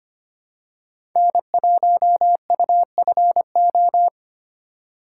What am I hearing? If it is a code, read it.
Morse code, heard as a keyed tone, about 25 wpm, 700 Hz. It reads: N1UFO
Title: USA Call Signs - 5 Characters - 2x Character Spacing - 25wpm